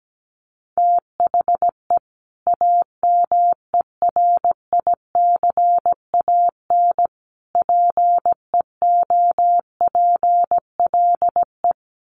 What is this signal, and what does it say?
Morse code, heard as a keyed tone, about 17 wpm, 700 Hz. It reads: THE AMERICAN PEOPLE
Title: Sets of 3 Words 17wpm